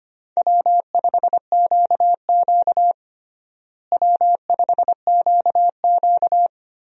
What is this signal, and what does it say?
Morse code, heard as a keyed tone, about 25 wpm, 700 Hz. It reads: W5QQ W5QQ
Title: USA Call Signs - 4 Characters - Repeated 3x - 25wpm